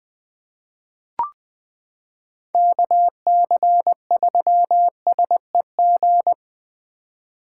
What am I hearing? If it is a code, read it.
Morse code, heard as a keyed tone, about 20 wpm, 700 Hz. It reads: KC3SEG
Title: 2023 POTA Activator Call Signs - 20wpm